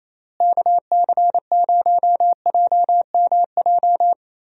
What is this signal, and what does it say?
Morse code, heard as a keyed tone, about 28 wpm, 700 Hz. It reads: KC0JMJ
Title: USA Call Signs - 6 Characters - Repeated 3x - 28wpm